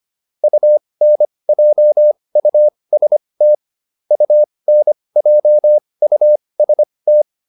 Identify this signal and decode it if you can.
Morse code, heard as a keyed tone, about 25 wpm, spaced as for 20 wpm, 600 Hz. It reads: UNJUST UNJUST
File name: UN6 20